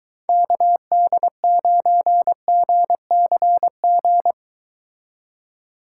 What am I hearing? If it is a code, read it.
Morse code, heard as a keyed tone, about 23 wpm, 700 Hz. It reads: KD9GCG